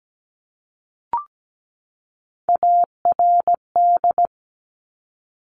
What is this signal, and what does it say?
Morse code, heard as a keyed tone, about 17 wpm, 700 Hz. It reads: ARD
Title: Top Three Letter Combinations in Words 17wpm